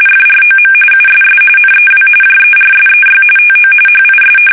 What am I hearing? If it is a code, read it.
RTTY, about 45.45 baud. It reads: FROM USS NEW JERSEY TO HIS
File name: fsk850 / battleshipnj